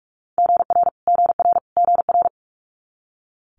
Morse code, heard as a keyed tone, about 46 wpm, 700 Hz. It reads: GR GR GR
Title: Multi-letter Phonemes Repeated 3x 45wpm